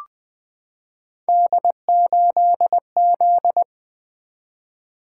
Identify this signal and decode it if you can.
Morse code, heard as a keyed tone, about 20 wpm, 700 Hz. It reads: D8Z
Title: Three Letter-Number Combinations - 20wpm